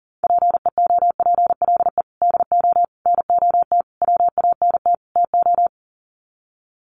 Morse code, heard as a keyed tone, about 40 wpm, 700 Hz. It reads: PEOPLE DO NOT WANT TO
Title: Sets of 5 Words Encore2 40wpm